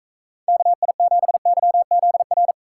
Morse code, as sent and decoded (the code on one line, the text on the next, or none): -.- .. --... -.-- --.. .-.
KI7YZR